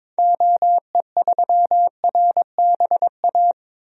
Morse code, heard as a keyed tone, about 22 wpm, 700 Hz. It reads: OE3RBA
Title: DXCC-100 Call Signs - 6 Characters - Repeated 3x - 22wpm